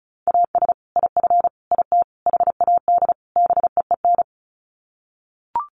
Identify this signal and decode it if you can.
Morse code, heard as a keyed tone, about 34 wpm, 700 Hz. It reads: AS IF IT HAD BEEN
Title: Sets of 5 Words Encore2 35wpm